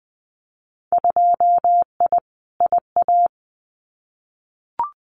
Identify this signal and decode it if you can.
Morse code, heard as a keyed tone, about 20 wpm, 700 Hz. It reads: 2I IA
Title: Winter Field Day Exchanges Encore 20wpm